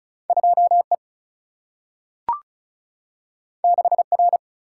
Morse code, heard as a keyed tone, about 36 wpm, 700 Hz. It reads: 2 E 6 R
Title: Single-Letter Single-Number 35wpm